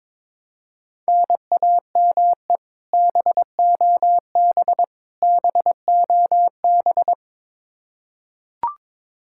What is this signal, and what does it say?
Morse code, heard as a keyed tone, about 22 wpm, 700 Hz. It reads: NAME BOB BOB
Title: K1USN SST QSO Practice 22wpm